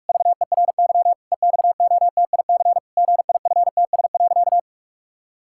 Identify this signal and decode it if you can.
Morse code, heard as a keyed tone, about 46 wpm, 700 Hz. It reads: VERY EXOTIC GIFTS.
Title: Pangram Challenge v2 - 45wpm